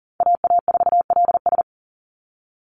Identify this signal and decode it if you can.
Morse code, heard as a keyed tone, about 40 wpm, 700 Hz. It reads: AA4LS
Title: Call Signs Repeated 2x - 40wpm